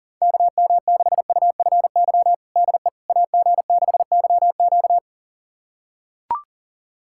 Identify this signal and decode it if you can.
Morse code, heard as a keyed tone, about 40 wpm, 700 Hz. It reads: KM6UFY DE AG6YQ
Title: USA Callsign DE Callsign 40wpm